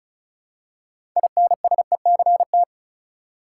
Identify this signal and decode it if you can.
Morse code, heard as a keyed tone, about 36 wpm, 700 Hz. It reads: INSECT
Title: Six Letter Words - 35wpm